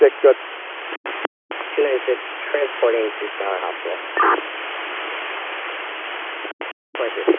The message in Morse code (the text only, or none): none